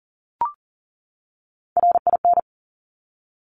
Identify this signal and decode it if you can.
Morse code, heard as a keyed tone, about 42 wpm, 690 Hz. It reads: RIN
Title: Top Three Letter Combinations in Words 40wpm